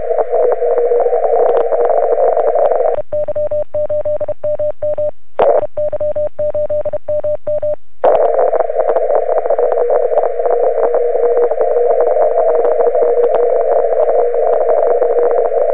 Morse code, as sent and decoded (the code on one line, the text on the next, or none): none